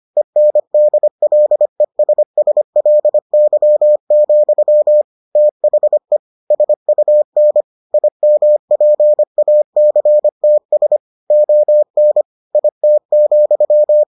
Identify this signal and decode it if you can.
Morse code, heard as a keyed tone, about 25 wpm, 600 Hz. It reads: ENDLESSLY, THE SUN IMPACTS ON IT,